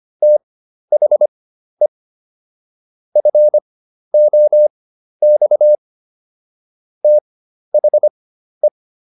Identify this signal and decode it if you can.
Morse code, heard as a keyed tone, about 25 wpm, spaced as for 12 wpm, 600 Hz. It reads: THE FOX THE